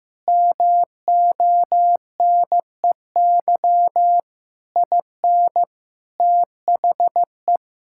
Morse code, heard as a keyed tone, about 15 wpm, 700 Hz. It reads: MONEY IN THE
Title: Sentences from Top 1000 Words - Review Part 6 - 15wpm